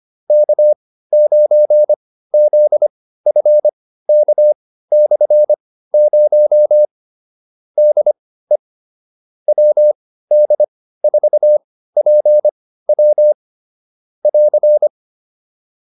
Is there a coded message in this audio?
Yes: Morse code from 0.3 to 14.9 s, heard as a keyed tone, about 25 wpm, spaced as for 15 wpm, 600 Hz. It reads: K9ZFK/0 DE WD4PW <AR>